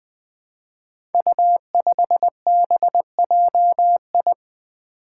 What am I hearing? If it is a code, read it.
Morse code, heard as a keyed tone, about 20 wpm, 700 Hz. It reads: U5BJI
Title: Five Challenging Letter-Number Combinations 20wpm